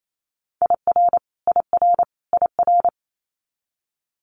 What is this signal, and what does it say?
Morse code, heard as a keyed tone, about 28 wpm, 700 Hz. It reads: IR IR IR
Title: Multi-letter Phonemes Repeated 3x 28wpm